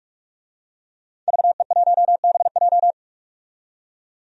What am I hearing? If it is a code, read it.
Morse code, heard as a keyed tone, about 42 wpm, 700 Hz. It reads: VE1BJ
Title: DXCC-100 Call Signs - 5 Characters - 45wpm